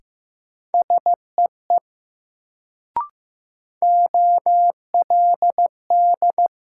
Morse code, heard as a keyed tone, about 15 wpm, 700 Hz. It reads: SEE OLD
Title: Sets of 2 Words - New Word Drill - 15wpm